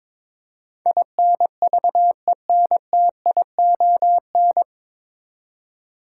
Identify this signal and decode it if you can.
Morse code, heard as a keyed tone, about 22 wpm, 700 Hz. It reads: INVENTION
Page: Morse Code Ninja practice 22wpm